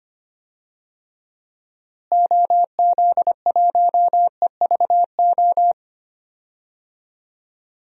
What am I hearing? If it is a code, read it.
Morse code, heard as a keyed tone, about 25 wpm, 700 Hz. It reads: OZ1EVO
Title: DXCC-100 Call Signs - 6 Characters - Repeated 2x - 25wpm